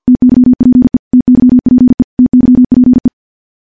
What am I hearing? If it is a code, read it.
Morse code, heard as a keyed tone, about 50 wpm, 250 Hz. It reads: TYPE TYPE TYPE
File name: morse / morse-training-50WPM